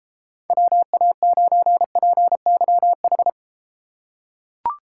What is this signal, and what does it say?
Morse code, heard as a keyed tone, about 32 wpm, 700 Hz. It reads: WA9PYH